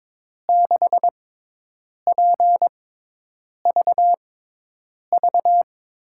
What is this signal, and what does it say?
Morse code, heard as a keyed tone, about 21 wpm, spaced as for 8 wpm, 700 Hz. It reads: 6PVV
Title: USA Call Signs - 6 Characters - 6x Character Spacing - 22wpm